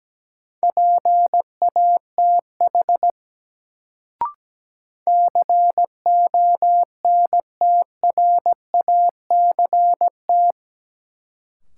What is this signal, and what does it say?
Morse code, heard as a keyed tone, about 17 wpm, 700 Hz. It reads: PATH CONTRACT
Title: Sentences from Top 1400 Words - New Word Drill Alt - 17wpm